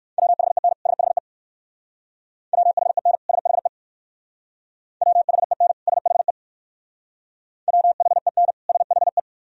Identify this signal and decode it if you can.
Morse code, heard as a keyed tone, about 58 wpm, 700 Hz. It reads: WHEN SHE WHEN SHE WHEN SHE WHEN SHE